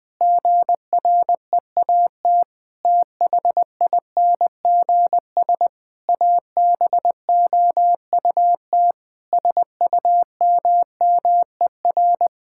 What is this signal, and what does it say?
Morse code, heard as a keyed tone, about 20 wpm, 700 Hz. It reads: GREAT THINGS ABOUT SUMMER
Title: Sentences with 6-Word Phrases - 20wpm